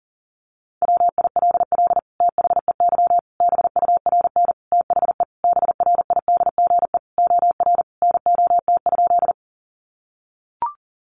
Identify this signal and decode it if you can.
Morse code, heard as a keyed tone, about 40 wpm, 700 Hz. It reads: WILL THEY BURN THE BRIDGE OR NOT?
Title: Sentences from Top 1300 Words 40wpm